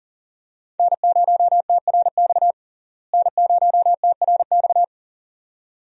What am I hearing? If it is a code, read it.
Morse code, heard as a keyed tone, about 40 wpm, 700 Hz. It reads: N0TRX N0TRX
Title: USA Call Signs - 5 Characters - Repeated 2x - 40wpm